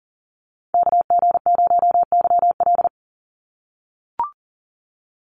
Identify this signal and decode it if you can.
Morse code, heard as a keyed tone, about 40 wpm, 700 Hz. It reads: KG0YL
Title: Call Signs - 40wpm